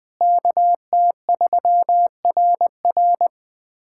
Morse code, heard as a keyed tone, about 21 wpm, 700 Hz. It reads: KT3RR
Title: K1USN SST QSO Practice - 20wpm